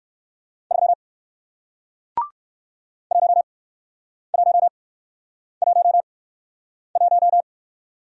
Morse code, heard as a keyed tone, about 60 wpm, spaced as for 9 wpm, 700 Hz. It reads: 3 1111